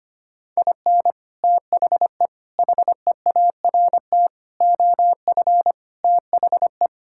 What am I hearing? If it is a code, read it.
Morse code, heard as a keyed tone, about 25 wpm, 700 Hz. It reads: IN THE HEART OF THE
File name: practice / 25wpm / Sentences with 3-Word Phrases - 25wpm